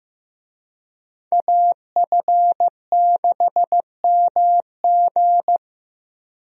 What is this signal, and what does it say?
Morse code, heard as a keyed tone, about 16 wpm, 700 Hz. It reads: AF6MG
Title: USA Call Signs - 5 Characters - Repeated 3x - 15wpm